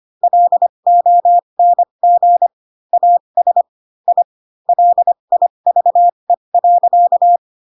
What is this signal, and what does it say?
Morse code, heard as a keyed tone, about 25 wpm, spaced as for 22 wpm, 700 Hz. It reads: LONG AS I LIVE.
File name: Pinocchio / Pinocchio ch7 18 25 ik1hge